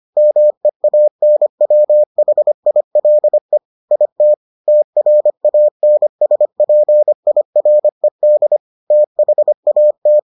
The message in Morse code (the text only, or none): MEANWHILE IT TRANSPIRED THAT